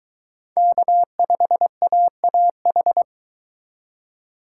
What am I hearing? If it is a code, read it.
Morse code, heard as a keyed tone, about 22 wpm, 700 Hz. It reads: K5AAH